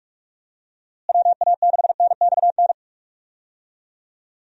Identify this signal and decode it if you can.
Morse code, heard as a keyed tone, about 46 wpm, 700 Hz. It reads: WA6NXN
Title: USA Call Signs - 6 Characters - Repeated 2x - 45wpm